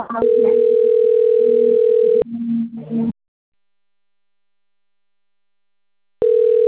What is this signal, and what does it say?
Ringback tone (North America), 440 + 480 Hz.